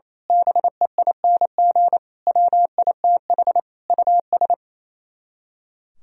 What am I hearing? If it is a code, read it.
Morse code, heard as a keyed tone, about 26 wpm, 700 Hz. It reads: BEING WITH US
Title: Speed-Racing - Sets of 5 Words 28wpm